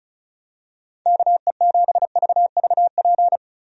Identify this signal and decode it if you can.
Morse code, heard as a keyed tone, about 34 wpm, 700 Hz. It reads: KE7VVP
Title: USA Call Signs - 6 Characters - Repeated 3x - 35wpm